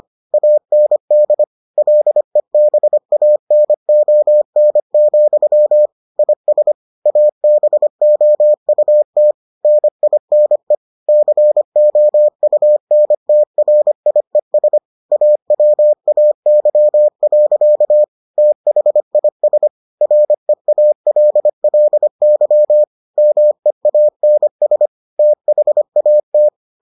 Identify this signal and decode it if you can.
Morse code, heard as a keyed tone, about 25 wpm, 600 Hz. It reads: AND LEBANON, IS ABOUT NINE COUNTRIES AWAY. THIS REALLY MEANS THAT